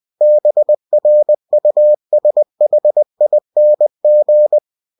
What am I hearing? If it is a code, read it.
Morse code, heard as a keyed tone, about 20 wpm, 600 Hz. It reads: BRUSHING
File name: cw-books / winnie-the-pooh / winnie-20wpm-page-0021